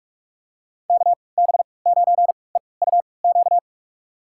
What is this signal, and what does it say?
Morse code, heard as a keyed tone, about 46 wpm, spaced as for 25 wpm, 700 Hz. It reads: KB9EUQ
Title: USA Call Signs - 6 Characters - 3x Character Spacing - 45wpm